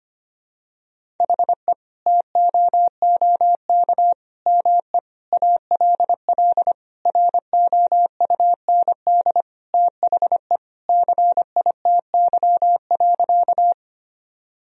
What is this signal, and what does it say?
Morse code, heard as a keyed tone, about 25 wpm, 700 Hz. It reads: HE TOOK ME ALL ROUND THE CITY.
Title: Sentences from Top 1000 Words - Review Part 1 - 25wpm